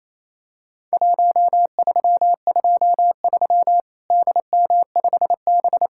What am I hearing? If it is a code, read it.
Morse code, heard as a keyed tone, about 27 wpm, 700 Hz. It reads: 1323 DM5B
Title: Worked All Europe DX Contest - QTC Exchange - 28wpm